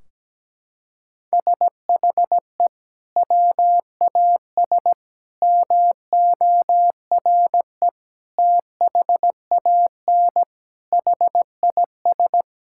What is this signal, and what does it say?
Morse code, heard as a keyed tone, about 17 wpm, 700 Hz. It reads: SHE WAS MORE THAN HIS
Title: Sentences from Top 1600 Words 17wpm